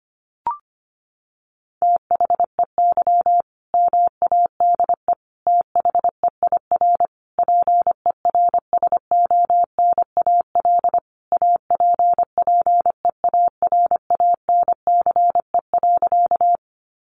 Morse code, heard as a keyed tone, about 25 wpm, 700 Hz. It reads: THEY MADE THEIR PERSONAL APPEARANCE.
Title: Sentences from Top 1100 Words 25wpm